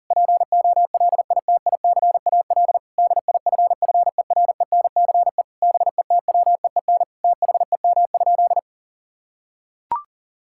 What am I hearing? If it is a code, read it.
Morse code, heard as a keyed tone, about 40 wpm, 700 Hz. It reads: POLITICAL DIFFERENCE BETWEEN THEM?
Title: Sentences from Top 1000 Words - Review Part 5 - 40wpm